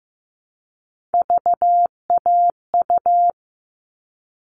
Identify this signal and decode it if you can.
Morse code, heard as a keyed tone, about 15 wpm, 700 Hz. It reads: VAU
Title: All Cut Numbers - 3-Digit 15wpm